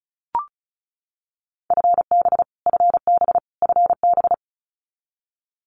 Morse code, heard as a keyed tone, about 34 wpm, 700 Hz. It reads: FB FB FB
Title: Abbreviations Repeated 3x - 35wpm